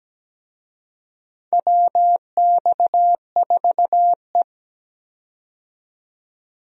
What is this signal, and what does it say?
Morse code, heard as a keyed tone, about 17 wpm, 700 Hz. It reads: WX4E
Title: USA Call Signs - 4 Characters - 17wpm